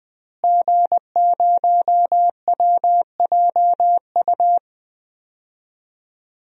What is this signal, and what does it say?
Morse code, heard as a keyed tone, about 20 wpm, 700 Hz. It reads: G0WJU
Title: DXCC-100 Call Signs - 5 Characters - Repeated 3x - 20wpm